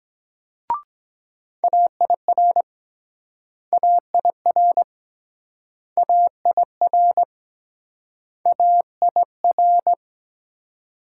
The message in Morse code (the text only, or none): AIR AIR AIR AIR